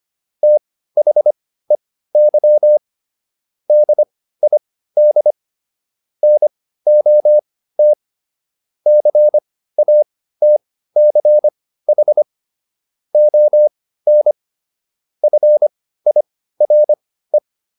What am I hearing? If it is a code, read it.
Morse code, heard as a keyed tone, about 25 wpm, spaced as for 15 wpm, 600 Hz. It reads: THEY DID NOT CATCH ON FIRE